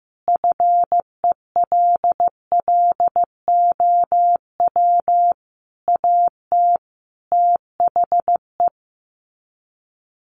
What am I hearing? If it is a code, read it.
Morse code, heard as a keyed tone, about 15 wpm, 700 Hz. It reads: FELLOW AT THE
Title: Sets of 5 Words Encore 15wpm